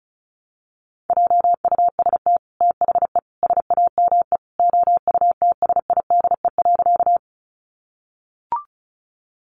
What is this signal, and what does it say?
Morse code, heard as a keyed tone, about 34 wpm, 700 Hz. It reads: JUST THE SAME OUTSIDE.
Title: Sentences from Top 900 Words 35wpm